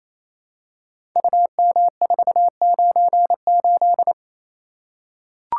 Morse code, heard as a keyed tone, about 27 wpm, 700 Hz. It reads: UM498
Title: Code Groups - 5 Letter:Numbers - 28wpm